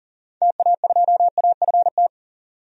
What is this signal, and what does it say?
Morse code, heard as a keyed tone, about 40 wpm, 700 Hz. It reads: TA2AFT
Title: DXCC-100 Call Signs - 6 Characters - 40wpm